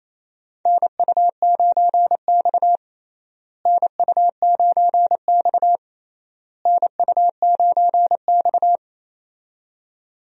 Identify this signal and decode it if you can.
Morse code, heard as a keyed tone, about 28 wpm, 700 Hz. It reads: NU9X NU9X NU9X
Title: USA Call Signs - 4 Characters - Repeated 3x - 28wpm